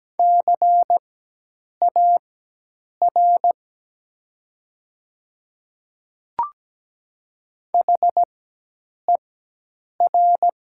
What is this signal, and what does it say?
Morse code, heard as a keyed tone, about 17 wpm, spaced as for 8 wpm, 700 Hz. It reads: CAR HER
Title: 4x Character Spacing - Top 500 Words - 17wpm